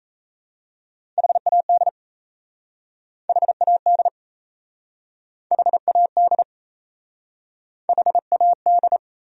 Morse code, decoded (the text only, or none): HAD HAD HAD HAD